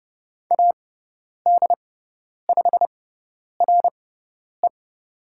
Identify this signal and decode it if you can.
Morse code, heard as a keyed tone, about 30 wpm, spaced as for 10 wpm, 700 Hz. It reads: AD5RE
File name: Lesson 16g - Letter c - Call Signs 30wpm